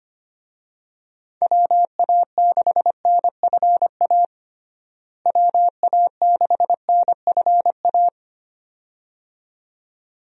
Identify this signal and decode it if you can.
Morse code, heard as a keyed tone, about 25 wpm, 700 Hz. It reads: WA6NFA WA6NFA
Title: USA Call Signs - 6 Characters - Repeated 2x - 25wpm